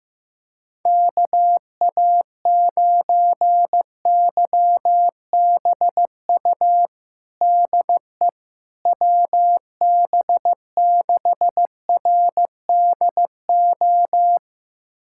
Morse code, heard as a keyed tone, about 15 wpm, 700 Hz. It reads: KA9YBU DE WB6RDO
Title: USA Callsign DE Callsign 15wpm